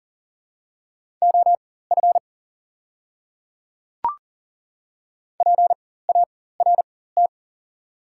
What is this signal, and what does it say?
Morse code, heard as a keyed tone, about 41 wpm, spaced as for 19 wpm, 690 Hz. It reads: OF PART
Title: 4x Character Spacing - Top 300 Words - 40wpm